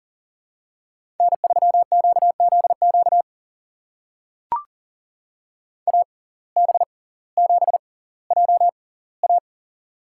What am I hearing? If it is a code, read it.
Morse code, heard as a keyed tone, about 39 wpm, spaced as for 20 wpm, 700 Hz. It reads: N3QZQ AB7JA
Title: USA Call Signs - 5 Characters - 6x Character Spacing - 40wpm